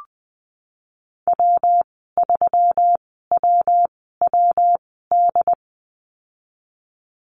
Morse code, heard as a keyed tone, about 20 wpm, spaced as for 15 wpm, 700 Hz. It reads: W3WWD
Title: 128 Lesson 21g - Number 6 - Call Signs 20wpm